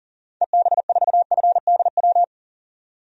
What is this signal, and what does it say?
Morse code, heard as a keyed tone, about 40 wpm, 700 Hz. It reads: EB4FDW
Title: DXCC-100 Call Signs - 6 Characters - 40wpm